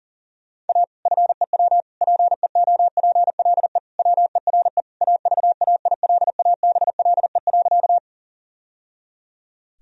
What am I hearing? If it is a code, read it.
Morse code, heard as a keyed tone, about 40 wpm, 700 Hz. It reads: A FEW PEOPLE WERE AVAILABLE.
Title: Sentences from Top 1000 Words - Review Part 3 - 40wpm